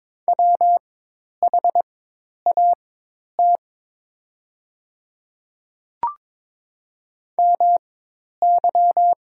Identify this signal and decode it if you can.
Morse code, heard as a keyed tone, about 22 wpm, spaced as for 10 wpm, 700 Hz. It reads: WHAT MY
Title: 4x Character Spacing - Top 100 Words - 22wpm